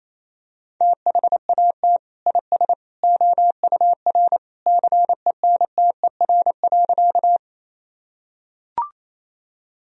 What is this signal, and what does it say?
Morse code, heard as a keyed tone, about 28 wpm, 700 Hz. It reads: THAT IS OUR CENTER.
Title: Sentences from Top 1000 Words - Review Part 2 - 28wpm